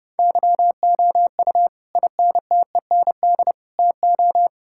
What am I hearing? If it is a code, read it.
Morse code, heard as a keyed tone, about 30 wpm, 700 Hz. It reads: YOU INTEND TO
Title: Sentences from Top 1600 Words 30wpm